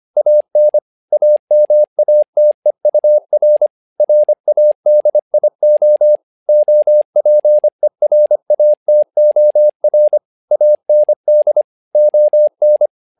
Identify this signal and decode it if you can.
Morse code, heard as a keyed tone, about 25 wpm, 600 Hz. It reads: AN AMATEUR RADIO OPERATOR AND ON
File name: learning center / tower trouble / tower trouble 25wpm